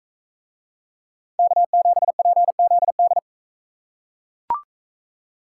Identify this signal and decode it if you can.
Morse code, heard as a keyed tone, about 41 wpm, 700 Hz. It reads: K7PZD